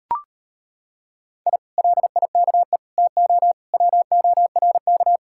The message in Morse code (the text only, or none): I LIKE TO WORK